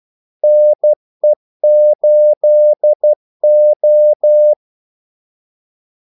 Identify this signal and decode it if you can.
Morse code, heard as a keyed tone, about 12 wpm, 600 Hz. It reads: NE8O
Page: 2 ekstra ordmellemrum mellem kaldesignalerne